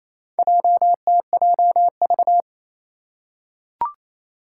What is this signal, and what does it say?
Morse code, heard as a keyed tone, about 28 wpm, 700 Hz. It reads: JTJV
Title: Four Challenging Letter-Number Combinations 28wpm